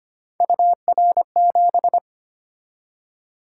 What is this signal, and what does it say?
Morse code, heard as a keyed tone, about 29 wpm, 700 Hz. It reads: UR7
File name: Speed-Racing - Three Letter-Number Combinations - 25wpm